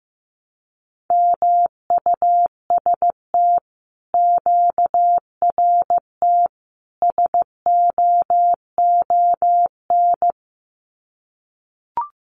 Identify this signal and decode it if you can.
Morse code, heard as a keyed tone, about 15 wpm, 700 Hz. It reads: MUST QRT SOON
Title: Common QSO Phrases Encore 15wpm